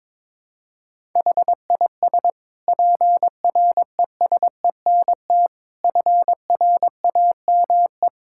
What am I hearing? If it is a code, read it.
Morse code, heard as a keyed tone, about 22 wpm, 700 Hz. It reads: HIS PRESENT FRAME
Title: Sentences from Top 1000 Words - Review Part 6 - 22wpm